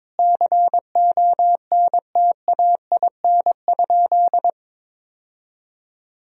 Morse code, heard as a keyed tone, about 21 wpm, 700 Hz. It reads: CONTAIN?
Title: Sentences from Top 1000 Words - Review Part 2 - 22wpm